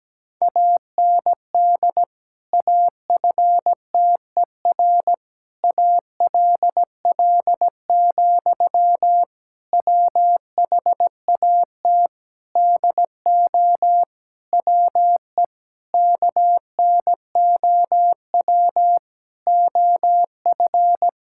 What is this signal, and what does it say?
Morse code, heard as a keyed tone, about 17 wpm, 700 Hz. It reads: AND AFTER ALL, WHAT DO WE KNOW OF
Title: Sentences from Top 1600 Words 17wpm